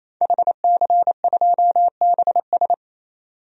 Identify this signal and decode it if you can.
Morse code, heard as a keyed tone, about 28 wpm, 700 Hz. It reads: HC2BS
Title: DXCC-100 Call Signs - 5 Characters - 28wpm